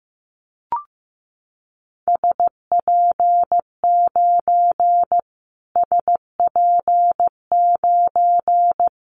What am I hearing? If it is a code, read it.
Morse code, heard as a keyed tone, about 15 wpm, 700 Hz. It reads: SP9 SP9
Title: Callsign Prefixes for Top 25 DXCC - Repeated 3x - 15wpm